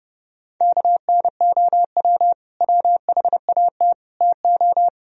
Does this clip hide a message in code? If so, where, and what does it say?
Yes: Morse code from 0.6 to 4.9 s, heard as a keyed tone, about 29 wpm, 700 Hz. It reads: KNOW WHAT TO